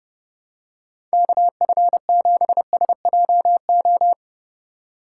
Morse code, heard as a keyed tone, about 29 wpm, 700 Hz. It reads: KF7SJO